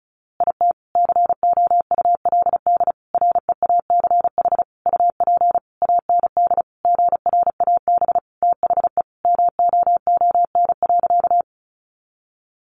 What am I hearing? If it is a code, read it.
Morse code, heard as a keyed tone, about 34 wpm, 700 Hz. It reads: IT COULD REACH UP AND GRAB THE MOON.
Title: A Sound Of Thunder 35wpm